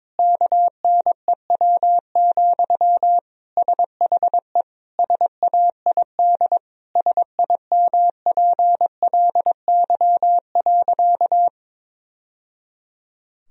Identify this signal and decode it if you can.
Morse code, heard as a keyed tone, about 22 wpm, 700 Hz. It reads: KNEW, SHE SAID SIMPLY.
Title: Sentences from Top 1000 Words - Review Part 4 - 22wpm